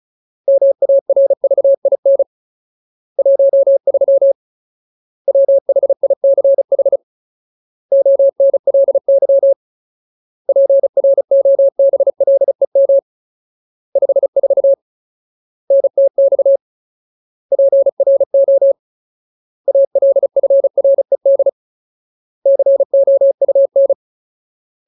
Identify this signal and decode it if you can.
Morse code, heard as a keyed tone, about 36 wpm, 550 Hz. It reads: MARVIN 13 WHICH ONLY PROBLEM 54 NTX PRO ALFRED COUN
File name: Ultimate Word List / 35WPM Ultimate